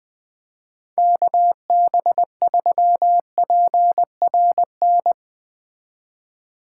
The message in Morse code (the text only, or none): KB3PRN